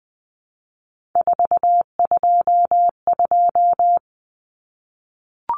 Morse code, heard as a keyed tone, about 20 wpm, 700 Hz. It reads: 422
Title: Numbers - 3-Digit 20wpm